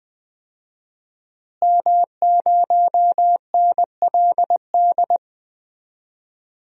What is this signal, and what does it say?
Morse code, heard as a keyed tone, about 20 wpm, 700 Hz. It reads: M0NLD
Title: DXCC-100 Call Signs - 5 Characters - Repeated 2x - 20wpm